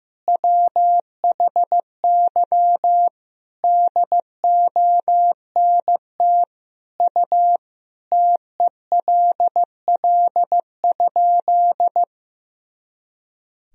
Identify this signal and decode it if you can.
Morse code, heard as a keyed tone, about 16 wpm, 700 Hz. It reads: WHY DONT U TELL?
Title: Sentences from Top 300 Words and Abbreviations 15wpm